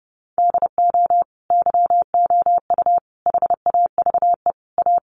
Morse code, heard as a keyed tone, about 30 wpm, 700 Hz. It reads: DO YOU HAVE A
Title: Sentences from Top 1400 Words 30wpm